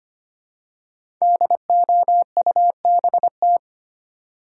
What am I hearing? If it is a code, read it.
Morse code, heard as a keyed tone, about 25 wpm, 700 Hz. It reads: DOUBT
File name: 5 Letter Words - First 3 Letters Given - 25wpm